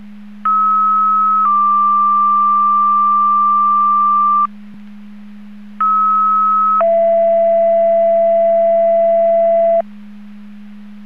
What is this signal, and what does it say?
Two-tone page, heard as a one-second tone, then a three-second tone, about 1290 then 1180 Hz, about 1290 then 670 Hz. It is heard from 0.4 to 9.8 s.
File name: Portage 154.385 Paging